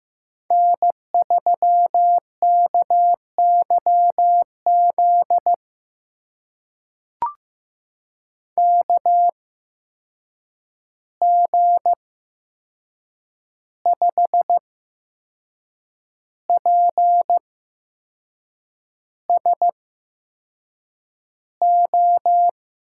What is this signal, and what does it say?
Morse code, heard as a keyed tone, about 15 wpm, spaced as for 6 wpm, 700 Hz. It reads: N3KYZ KG5PSO
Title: USA Call Signs - 8x Character Spacing - 15wpm